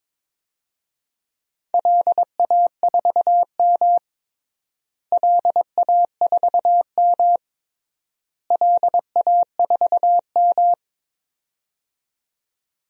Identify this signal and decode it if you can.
Morse code, heard as a keyed tone, about 22 wpm, 700 Hz. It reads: LA4M LA4M LA4M
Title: DXCC-100 Call Signs - 4 Characters - Repeated 3x - 22wpm